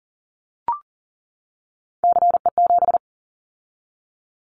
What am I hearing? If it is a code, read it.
Morse code, heard as a keyed tone, about 40 wpm, 700 Hz. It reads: CE7
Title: Callsign Prefixes for Top 100 DXCC - 40wpm